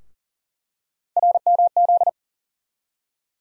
Morse code, heard as a keyed tone, about 40 wpm, 700 Hz. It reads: RMZ
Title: USA Call Sign Postfixes - 3 Letters 40wpm